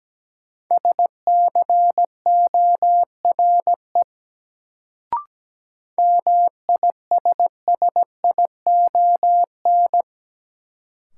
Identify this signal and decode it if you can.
Morse code, heard as a keyed tone, about 17 wpm, 700 Hz. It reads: SCORE MISSION
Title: Sentences from Top 1600 Words - New Word Drill - 17wpm